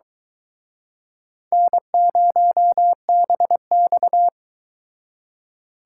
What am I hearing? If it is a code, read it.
Morse code, heard as a keyed tone, about 24 wpm, 700 Hz. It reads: N0BX